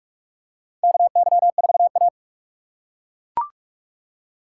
Morse code, heard as a keyed tone, about 42 wpm, 700 Hz. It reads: KY4A
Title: USA Call Signs - 4 Characters - 6x Character Spacing - 45wpm